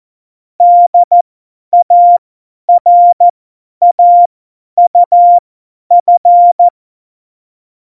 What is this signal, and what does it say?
Morse code, heard as a keyed tone, about 14 wpm, spaced as for 10 wpm, 700 Hz. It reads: DARAUF
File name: Morsekurs / Witze / Witz 14. November 2017, 50 F70